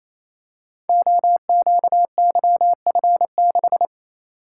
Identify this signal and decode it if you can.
Morse code, heard as a keyed tone, about 27 wpm, 700 Hz. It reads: OQYF6
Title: Code Groups - 5 Letter:Numbers - 28wpm